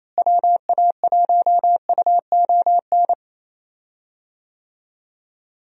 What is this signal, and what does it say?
Morse code, heard as a keyed tone, about 28 wpm, 700 Hz. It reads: WA1UON
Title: USA Call Signs - 6 Characters - Repeated 2x - 28wpm